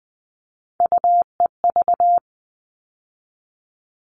Morse code, heard as a keyed tone, about 20 wpm, 700 Hz. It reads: UEV
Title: All Cut Numbers - 3-Digit 20wpm